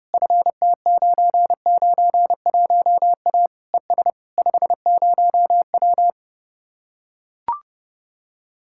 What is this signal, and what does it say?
Morse code, heard as a keyed tone, about 30 wpm, 700 Hz. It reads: FT991A ES 50W